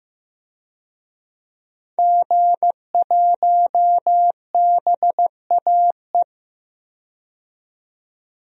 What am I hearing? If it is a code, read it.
Morse code, heard as a keyed tone, about 15 wpm, 700 Hz. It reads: G1BAE
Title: DXCC-100 Call Signs - 5 Characters - 15wpm